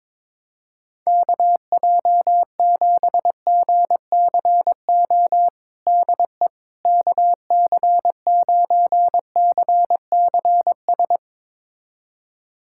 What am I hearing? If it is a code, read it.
Morse code, heard as a keyed tone, about 22 wpm, 700 Hz. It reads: KJ7GCO DE KC9CCS
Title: USA Callsign DE Callsign 22wpm